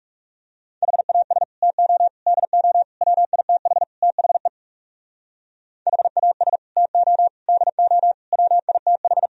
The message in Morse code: HAS TO DO WITH THE HAS TO DO WITH

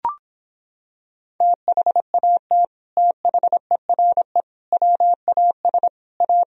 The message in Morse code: THAT THERE WAS A

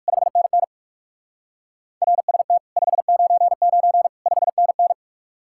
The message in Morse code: 5NN RST 599 5NN

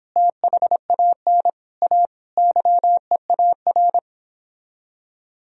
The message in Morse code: THAN A YEAR

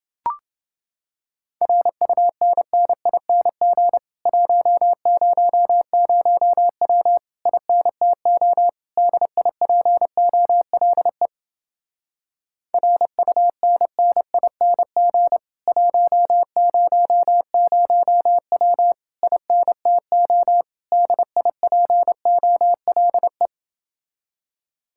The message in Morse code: RUNNING 100W INTO DIPOLE RUNNING 100W INTO DIPOLE

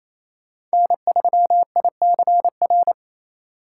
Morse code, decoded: N3ICR

